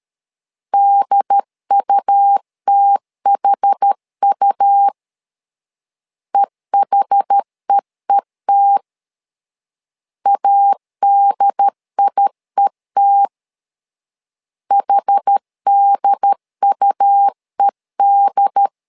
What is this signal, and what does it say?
Morse code, heard as a keyed tone, about 13 wpm, 800 Hz. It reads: DUTHU EHEET ADIET HDUED
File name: LEZIONI / 2 UHDN / 2 UHD 12wpm 1540